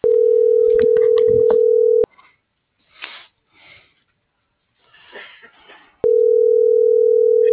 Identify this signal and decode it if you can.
Ringback tone (North America), 440 + 480 Hz.